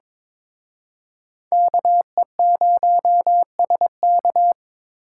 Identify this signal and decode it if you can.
Morse code, heard as a keyed tone, about 26 wpm, 700 Hz. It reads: KE0SK
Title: Speed Racing - USA 2x2 Call Signs 22wpm